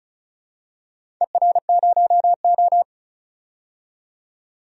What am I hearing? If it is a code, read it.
Morse code, heard as a keyed tone, about 36 wpm, 700 Hz. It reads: ER0O
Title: DXCC-100 Call Signs - 4 Characters - 35wpm